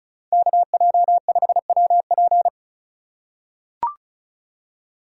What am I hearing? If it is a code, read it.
Morse code, heard as a keyed tone, about 36 wpm, 700 Hz. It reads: KJ5WP
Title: USA Call Signs - 5 Characters - 4x Character Spacing - 35wpm